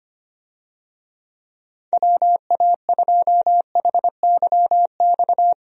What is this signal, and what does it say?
Morse code, heard as a keyed tone, about 25 wpm, 700 Hz. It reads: WA2HYX
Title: USA Call Signs - 6 Characters - 25wpm